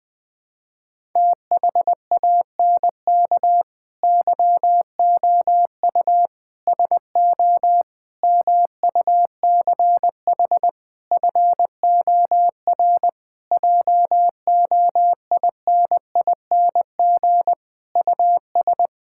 Morse code, heard as a keyed tone, about 20 wpm, 700 Hz. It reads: THANK YOU SO MUCH FOR JOINING US
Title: Sentences with 5-Word Phrases - 20wpm